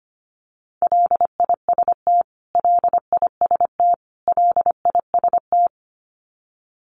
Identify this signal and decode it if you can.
Morse code, heard as a keyed tone, about 25 wpm, 700 Hz. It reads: LIST LIST LIST
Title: Top 500 Words Repeated 25wpm